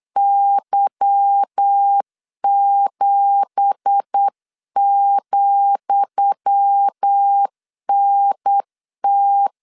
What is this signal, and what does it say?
Morse code, heard as a keyed tone, about 8 wpm, 800 Hz. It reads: Y7,NT
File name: LEZIONI / 10 SIGNS / 10D COMA 8wpm 3749